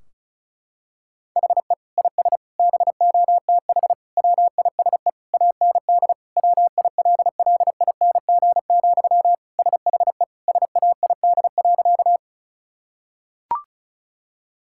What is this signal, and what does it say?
Morse code, heard as a keyed tone, about 36 wpm, 700 Hz. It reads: HE IS BOTH WISE AND WILLING, SHE SAID.
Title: Sentences from Top 1600 Words 35wpm